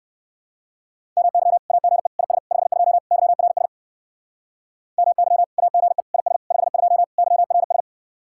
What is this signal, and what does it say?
Morse code, heard as a keyed tone, about 65 wpm, 700 Hz. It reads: MY AGE IS 52 YRS MY AGE IS 52 YRS